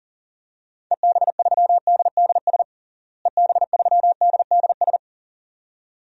Morse code, heard as a keyed tone, about 40 wpm, 700 Hz. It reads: EB3DDS EB3DDS
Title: DXCC-100 Call Signs - 6 Characters - Repeated 2x - 40wpm